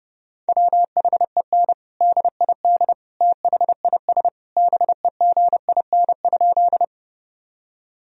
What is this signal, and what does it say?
Morse code, heard as a keyed tone, about 30 wpm, 700 Hz. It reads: WHEN DID THIS BEGIN?
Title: Sentences from Top 1000 Words - Review Part 1 - 30wpm